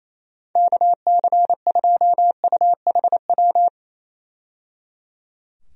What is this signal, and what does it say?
Morse code, heard as a keyed tone, about 28 wpm, 700 Hz. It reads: KC2UHW
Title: USA Call Signs - 6 Characters - 28wpm